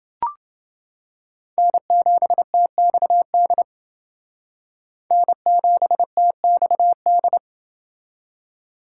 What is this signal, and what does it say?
Morse code, heard as a keyed tone, about 29 wpm, 700 Hz. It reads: N7TXD N7TXD